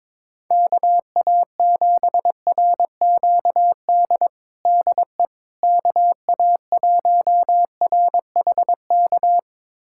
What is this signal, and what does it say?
Morse code, heard as a keyed tone, about 22 wpm, 700 Hz. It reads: KA7RQD DE KA1RHK
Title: USA Callsign DE Callsign - 6 Characters - 22wpm